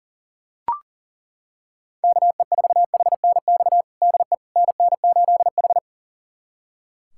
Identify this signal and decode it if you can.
Morse code, heard as a keyed tone, about 40 wpm, 700 Hz. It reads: KE4HNX DE NN8H
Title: USA Callsign DE Callsign 40wpm